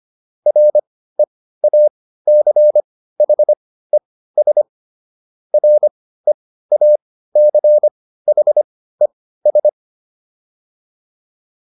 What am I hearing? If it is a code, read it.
Morse code, heard as a keyed tone, about 25 wpm, spaced as for 15 wpm, 600 Hz. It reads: REACHES REACHES